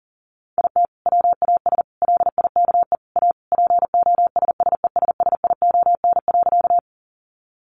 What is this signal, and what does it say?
Morse code, heard as a keyed tone, about 40 wpm, 700 Hz. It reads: IT WAS LIKE A POSSESSION.
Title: Sentences from Top 1400 Words 40wpm